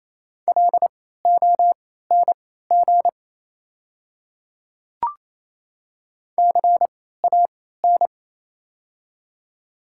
Morse code, heard as a keyed tone, about 28 wpm, spaced as for 16 wpm, 700 Hz. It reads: LONG CAN
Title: 3x Character Spacing - Top 100 Words - 28wpm